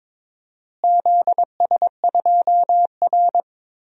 Morse code, heard as a keyed tone, about 22 wpm, 700 Hz. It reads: ZS2R
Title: DXCC-100 Call Signs - 4 Characters - Repeated 3x - 22wpm